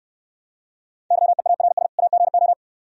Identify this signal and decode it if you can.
Morse code, heard as a keyed tone, about 69 wpm, 700 Hz. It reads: YEARS AGO